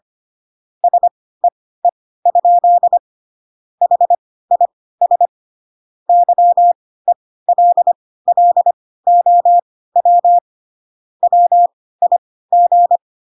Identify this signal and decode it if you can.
Morse code, heard as a keyed tone, about 25 wpm, spaced as for 16 wpm, 700 Hz. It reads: SEE? HIS YELLOW WIG
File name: Pinocchio / Pinocchio ch3 10 25 ik1hge